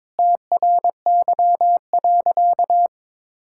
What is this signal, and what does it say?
Morse code, heard as a keyed tone, about 22 wpm, 700 Hz. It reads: TRY.
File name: practice / 22wpm / Sentences with 2-Word Phrases - 22wpm